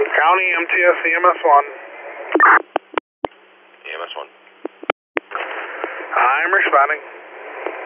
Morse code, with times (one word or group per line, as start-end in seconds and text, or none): none